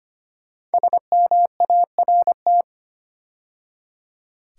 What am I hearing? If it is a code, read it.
Morse code, heard as a keyed tone, about 25 wpm, 700 Hz. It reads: SMART
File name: 5 Letter Words - First 3 Letters Given - 25wpm